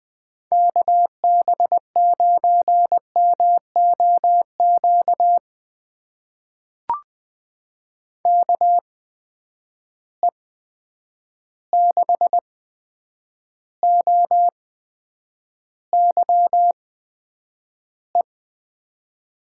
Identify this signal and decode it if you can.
Morse code, heard as a keyed tone, about 20 wpm, spaced as for 9 wpm, 700 Hz. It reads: KB9MOQ KE6OYE
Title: USA Call Signs - 6 Characters - 8x Character Spacing - 20wpm